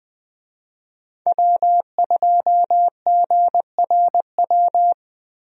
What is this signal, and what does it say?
Morse code, heard as a keyed tone, about 21 wpm, 700 Hz. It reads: W2GRW
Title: USA Call Signs - 5 Characters - Repeated 3x - 20wpm